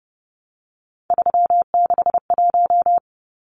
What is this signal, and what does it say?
Morse code, heard as a keyed tone, about 30 wpm, 700 Hz. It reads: 361
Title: Numbers - 3-Digit 30wpm